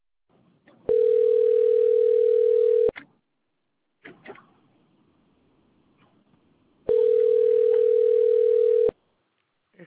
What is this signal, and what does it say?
Ringback tone (North America), 440 + 480 Hz.